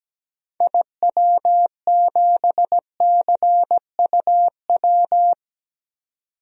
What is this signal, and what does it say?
Morse code, heard as a keyed tone, about 17 wpm, 700 Hz. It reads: IW7CUW
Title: DXCC-100 Call Signs - 6 Characters - 17wpm